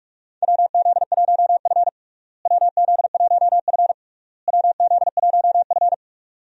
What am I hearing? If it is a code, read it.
Morse code, heard as a keyed tone, about 46 wpm, 700 Hz. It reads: WZ1F WZ1F WZ1F
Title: USA Call Signs - 4 Characters - Repeated 3x - 45wpm